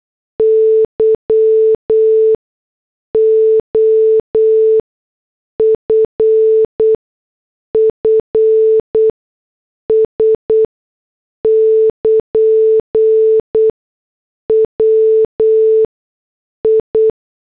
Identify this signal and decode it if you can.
Morse code, heard as a keyed tone, about 8 wpm, spaced as for 6 wpm, 440 Hz. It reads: YOFFS <KN> WI